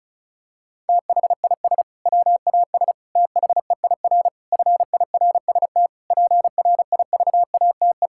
Morse code, heard as a keyed tone, about 34 wpm, 700 Hz. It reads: THIS WAS THEIR FIRST PRIVATE
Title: Sentences from Top 1600 Words 35wpm